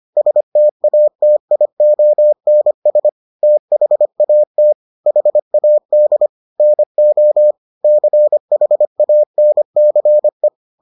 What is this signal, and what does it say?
Morse code, heard as a keyed tone, about 25 wpm, 600 Hz. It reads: STATIONS THAT HAD NO CHANCE